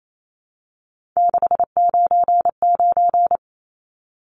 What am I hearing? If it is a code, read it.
Morse code, heard as a keyed tone, about 28 wpm, 700 Hz. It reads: 699
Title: Numbers - 3-Digit 28wpm